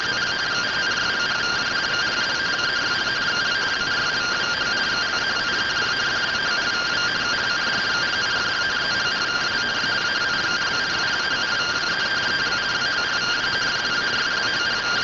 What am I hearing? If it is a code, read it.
Two-tone teleprinter signal, about 170 Hz shift.
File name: findings / CT AXIAL -572mm 2.gen-45 rtty rx